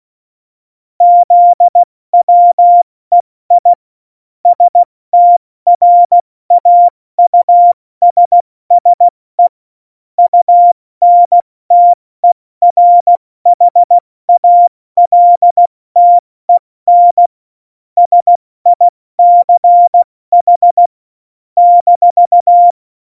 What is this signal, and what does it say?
Morse code, heard as a keyed tone, about 16 wpm, spaced as for 14 wpm, 700 Hz. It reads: ZWEI STRAUSSE UNTERHALTEN SICH -
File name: Morsekurs / Witze / Witz 01. Mai 2018, 70 F80